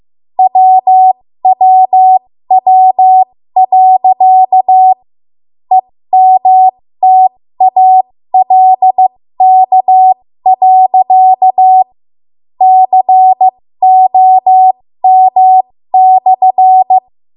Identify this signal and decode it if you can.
Morse code, heard as a keyed tone, about 15 wpm, spaced as for 12 wpm, 750 Hz. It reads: WWW.EMTALK.COM/
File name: Morse / 13WPM